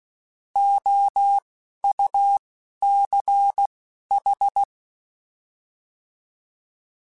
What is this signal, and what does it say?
Morse code, heard as a keyed tone, about 16 wpm, spaced as for 11 wpm, 800 Hz. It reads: OUCH
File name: Alphabetical / Alphabetical mp3s / Os @10